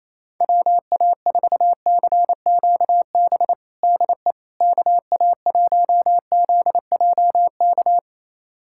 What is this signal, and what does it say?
Morse code, heard as a keyed tone, about 28 wpm, 700 Hz. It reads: WA4CQB DE KA1ZJK
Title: USA Callsign DE Callsign - 6 Characters - 28wpm